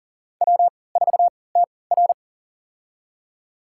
Morse code, heard as a keyed tone, about 39 wpm, spaced as for 22 wpm, 700 Hz. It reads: W4TR
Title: USA Call Signs - 4 Characters - 3x Character Spacing - 40wpm